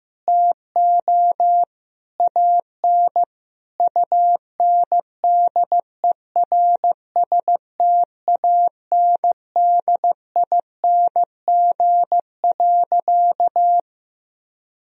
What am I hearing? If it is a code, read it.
Morse code, heard as a keyed tone, about 16 wpm, 700 Hz. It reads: TO AN UNDERSTANDING.
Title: Sentences from Top 1000 Words - Review Part 3 - 15wpm